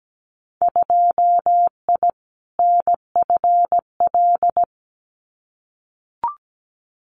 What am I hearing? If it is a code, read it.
Morse code, heard as a keyed tone, about 17 wpm, 700 Hz. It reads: 2I NFL
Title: Winter Field Day Exchanges 17wpm